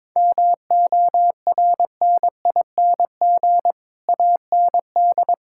Morse code, heard as a keyed tone, about 22 wpm, 700 Hz. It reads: MORNING AND
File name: Sets of 9 Words - 22wpm